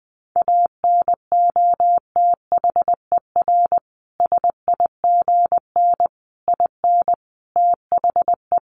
Morse code, heard as a keyed tone, about 20 wpm, 700 Hz. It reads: ANOTHER SIGN IN THE
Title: Sentences from Top 1000 Words 20wpm